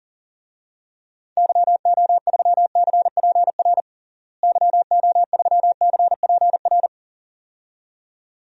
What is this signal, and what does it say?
Morse code, heard as a keyed tone, about 37 wpm, 700 Hz. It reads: YO3CPR YO3CPR
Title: DXCC-100 Call Signs - 6 Characters - Repeated 2x - 40wpm